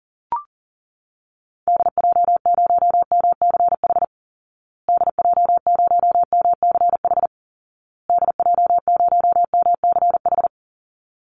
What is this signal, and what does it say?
Morse code, heard as a keyed tone, about 40 wpm, 700 Hz. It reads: DJ0MCH DJ0MCH DJ0MCH
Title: Call Signs Repeated 2x - 40wpm